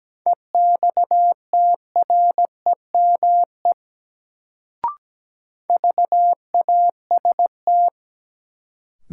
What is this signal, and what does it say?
Morse code, heard as a keyed tone, about 17 wpm, 700 Hz. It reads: EXTREME VAST
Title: Sentences from Top 1300 Words - New Word Drill - 17wpm